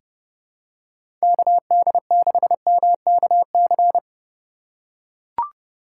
Morse code, heard as a keyed tone, about 30 wpm, 700 Hz. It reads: KD6MKC
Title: USA Call Signs - 6 Characters - 3x Character Spacing - 30wpm